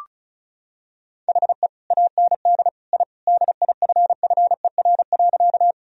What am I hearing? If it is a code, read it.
Morse code, heard as a keyed tone, about 34 wpm, 700 Hz. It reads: HE AND I DIFFER.
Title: Sentences from Top 1000 Words - Review Part 1 - 35wpm